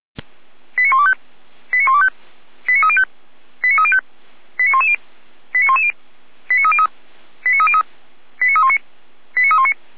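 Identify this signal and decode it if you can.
Five-tone selective call (ZVEI), heard as quick runs of five tones, tones of about 70 ms. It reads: 89125 89125 89305 89305 89110 89110 89303 89303 89319 89319
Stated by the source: ZVEI five-tone selective call received via SDR.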